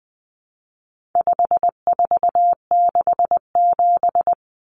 Morse code, heard as a keyed tone, about 20 wpm, 700 Hz. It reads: 5467
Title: Common Cut Numbers - 4-Digit 20wpm